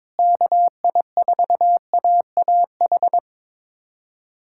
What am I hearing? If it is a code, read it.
Morse code, heard as a keyed tone, about 22 wpm, 700 Hz. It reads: KI4AAH
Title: USA Callsign DE Callsign - 6 Characters - 22wpm